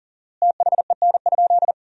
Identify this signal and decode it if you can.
Morse code, heard as a keyed tone, about 37 wpm, 700 Hz. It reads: THEN?
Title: 4x Word-Spacing - Sentences from Top 500 Words - 40wpm